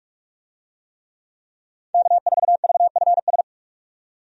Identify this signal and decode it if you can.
Morse code, heard as a keyed tone, about 46 wpm, 700 Hz. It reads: K4VFS
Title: USA Call Signs - 5 Characters - 45wpm